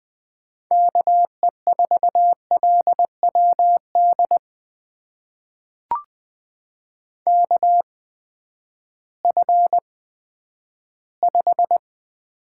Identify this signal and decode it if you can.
Morse code, heard as a keyed tone, about 20 wpm, spaced as for 11 wpm, 700 Hz. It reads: KE4LWD KF5
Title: USA Call Signs - 6 Characters - 8x Character Spacing - 20wpm